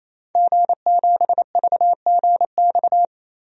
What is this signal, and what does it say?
Morse code, heard as a keyed tone, about 28 wpm, 700 Hz. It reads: G7VGX
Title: DXCC-100 Call Signs - 5 Characters - 28wpm